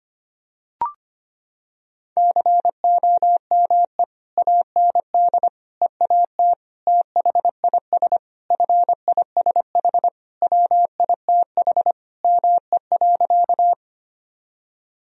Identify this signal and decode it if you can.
Morse code, heard as a keyed tone, about 25 wpm, 700 Hz. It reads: COME AND EAT THIS FISH WITH ME.
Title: Sentences from Top 1000 Words - Review Part 1 - 25wpm